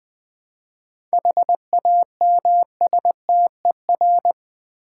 Morse code, heard as a keyed tone, about 20 wpm, 700 Hz. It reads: HAMSTER
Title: Easy Words with Hints - 20wpm